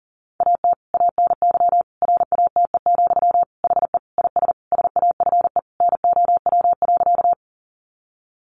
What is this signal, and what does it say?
Morse code, heard as a keyed tone, about 40 wpm, 700 Hz. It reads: AT ANY RATE, HE IS SAFE NOW.
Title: Sentences from Top 900 Words 40wpm